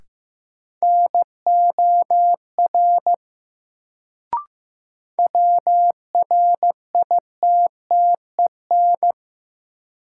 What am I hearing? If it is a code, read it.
Morse code, heard as a keyed tone, about 15 wpm, 700 Hz. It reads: NOR WRITTEN
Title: Sentences from Top 1500 Words - New Word Drill - 15wpm